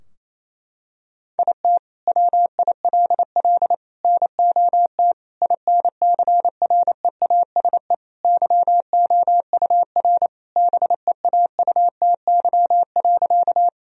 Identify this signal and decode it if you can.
Morse code, heard as a keyed tone, about 28 wpm, 700 Hz. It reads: IT WILL NOT INCREASE YOUR BEAUTY.
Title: Sentences from Top 1000 Words - Review Part 4 - 28wpm